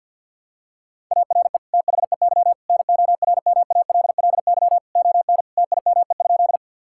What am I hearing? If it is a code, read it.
Morse code, heard as a keyed tone, about 50 wpm, 700 Hz. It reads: ARE THEY NORMALLY ON TIME?
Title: Sentences from Top 1000 Words - Review Part 6 - 50wpm